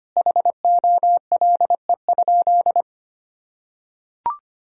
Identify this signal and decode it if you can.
Morse code, heard as a keyed tone, about 25 wpm, 700 Hz. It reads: HOLE?
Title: Sentences from Top 1600 Words 25wpm